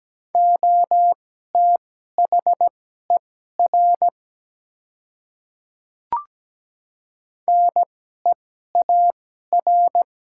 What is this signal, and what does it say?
Morse code, heard as a keyed tone, about 17 wpm, spaced as for 12 wpm, 700 Hz. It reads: OTHER NEAR
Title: 2x Character Spacing - Top 200 Words - 17wpm